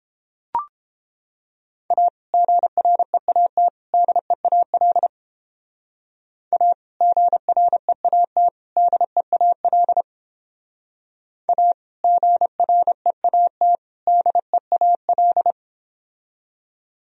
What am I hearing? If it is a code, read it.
Morse code, heard as a keyed tone, about 31 wpm, 700 Hz. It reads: A GREAT DEAL A GREAT DEAL A GREAT DEAL